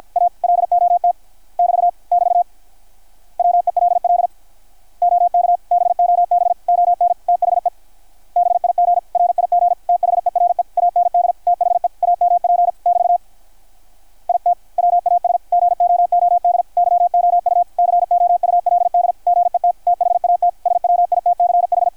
Morse code, heard as a keyed tone, about 53 wpm, 660 Hz, with radio noise. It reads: A LOT <BT> <BT> WELL OK BOB ON THE BIG RIG THERE AND THE AMP <BT> IT WAS GOOD YOU COULD GET THAT SWITCH
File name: archived / CRSnet